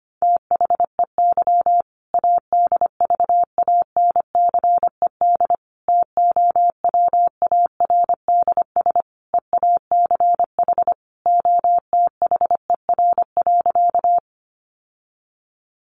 Morse code, heard as a keyed tone, about 25 wpm, 700 Hz. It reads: THEY ADVANCED TOWARDS EACH OTHER.
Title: Sentences from Top 1200 Words 25wpm